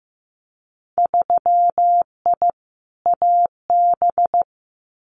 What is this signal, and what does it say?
Morse code, heard as a keyed tone, about 15 wpm, 700 Hz. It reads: 3I AB
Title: Winter Field Day Exchanges Encore 15wpm